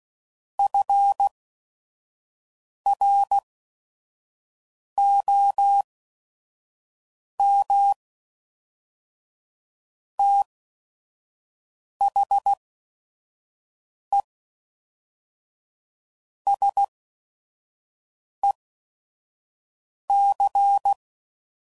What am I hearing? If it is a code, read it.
Morse code, heard as a keyed tone, about 16 wpm, spaced as for 5 wpm, 800 Hz. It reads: FROM THE SEC
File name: CW Code Groups @5WPM